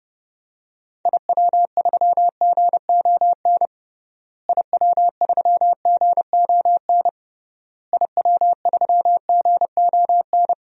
Morse code, heard as a keyed tone, about 32 wpm, 700 Hz. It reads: IW3GON IW3GON IW3GON
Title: DXCC-100 Call Signs - 6 Characters - Repeated 3x - 30wpm